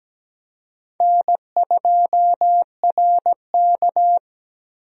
Morse code, heard as a keyed tone, about 17 wpm, 700 Hz. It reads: N2RK
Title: USA Call Signs - 4 Characters - 17wpm